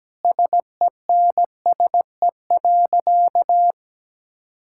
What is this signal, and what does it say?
Morse code, heard as a keyed tone, about 17 wpm, 700 Hz. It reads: SENSE.
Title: Sentences from Top 1000 Words - Review Part 5 - 17wpm